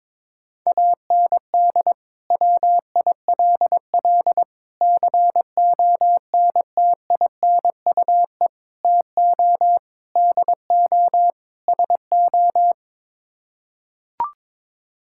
Morse code, heard as a keyed tone, about 22 wpm, 700 Hz. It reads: AND WILL CONTINUE TO DO SO
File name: practice / 22wpm / Sets of 6 Words - 22wpm